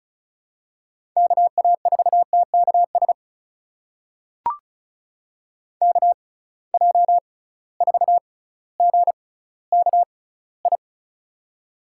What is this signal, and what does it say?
Morse code, heard as a keyed tone, about 36 wpm, spaced as for 18 wpm, 700 Hz. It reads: KA4TKS KJ4GKI
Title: USA Call Signs - 6 Characters - 6x Character Spacing - 35wpm